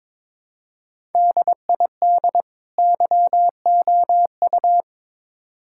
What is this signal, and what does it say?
Morse code, heard as a keyed tone, about 22 wpm, 700 Hz. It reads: DID YOU
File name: Build A Phrase - 2-Word Phrases - 22wpm